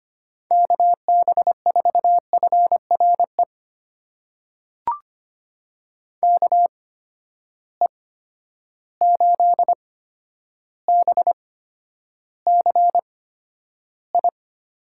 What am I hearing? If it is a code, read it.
Morse code, heard as a keyed tone, about 25 wpm, spaced as for 11 wpm, 700 Hz. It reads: KB4FRE KE8BCI
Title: USA Call Signs - 6 Characters - 8x Character Spacing - 25wpm